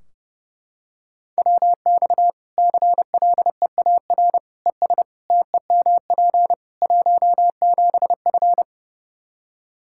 Morse code, heard as a keyed tone, about 30 wpm, 700 Hz. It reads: WX CLEAR ES TEMP 17F
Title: Practice QSO - Exchange 2 - Info - 30wpm